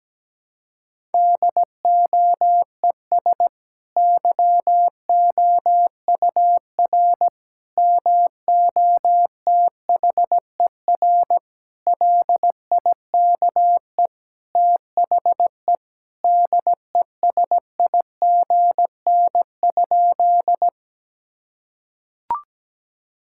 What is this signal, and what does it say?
Morse code, heard as a keyed tone, about 17 wpm, 700 Hz. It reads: DOES YOUR MOTHER LIKE THE DESIGN?
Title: Sentences from Top 1000 Words - Review Part 5 - 17wpm